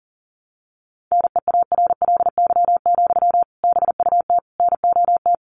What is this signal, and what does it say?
Morse code, heard as a keyed tone, about 40 wpm, 700 Hz. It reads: NEARLY, BUT NOT
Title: Sentences from Top 1300 Words 40wpm